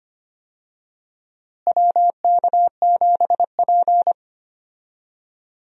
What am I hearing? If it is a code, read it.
Morse code, heard as a keyed tone, about 25 wpm, 700 Hz. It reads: WK7P
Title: USA Call Signs - 4 Characters - Repeated 2x - 25wpm